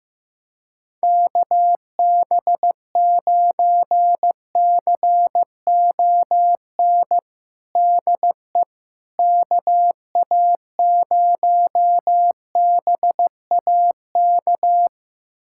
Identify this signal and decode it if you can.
Morse code, heard as a keyed tone, about 15 wpm, 700 Hz. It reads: KB9CON DE KA0BAK
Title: USA Callsign DE Callsign 15wpm